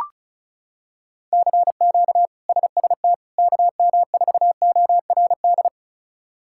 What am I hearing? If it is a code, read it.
Morse code, heard as a keyed tone, about 36 wpm, 700 Hz. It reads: CQ SST KM4ORD
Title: K1USN SST QSO Practice - 35wpm